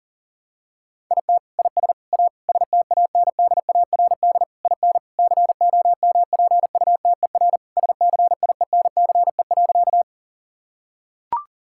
Morse code, heard as a keyed tone, about 40 wpm, 700 Hz. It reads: IT IS A STANDARD IN COMPUTER SCIENCE.
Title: Sentences from Top 1000 Words - Review Part 6 - 40wpm